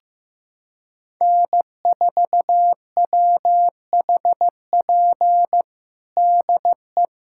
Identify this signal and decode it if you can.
Morse code, heard as a keyed tone, about 15 wpm, 700 Hz. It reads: N4WHP DE
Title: USA Callsign DE Callsign - 5 Characters - 15wpm